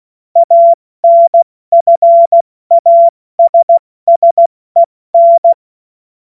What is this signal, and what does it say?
Morse code, heard as a keyed tone, about 16 wpm, 680 Hz. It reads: ANFASSEN
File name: Morsekurs / Witze / Witz 17. August 2021, 70 F80